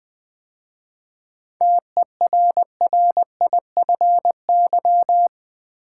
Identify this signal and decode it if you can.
Morse code, heard as a keyed tone, about 20 wpm, 700 Hz. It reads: TERRIFY
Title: 7 Letter Words - First 3 Letters Given - 20wpm